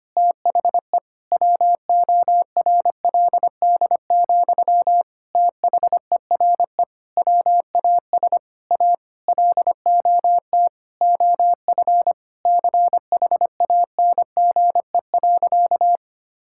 Morse code, heard as a keyed tone, about 25 wpm, 700 Hz. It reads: THE WORLD, THERE WAS A LOT OF CHANGE.
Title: Sentences from Top 1000 Words - Review Part 5 - 25wpm